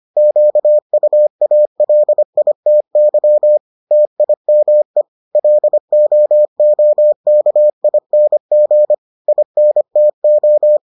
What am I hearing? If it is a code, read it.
Morse code, heard as a keyed tone, about 25 wpm, 600 Hz. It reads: QUALITY TIME LOOKING INTO